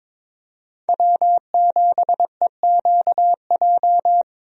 Morse code, heard as a keyed tone, about 22 wpm, 700 Hz. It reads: W7EQJ
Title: USA Call Signs - 5 Characters - Repeated 2x - 22wpm